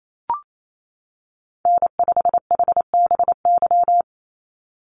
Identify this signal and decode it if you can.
Morse code, heard as a keyed tone, about 28 wpm, 700 Hz. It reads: N5HBY
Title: USA Call Signs 28wpm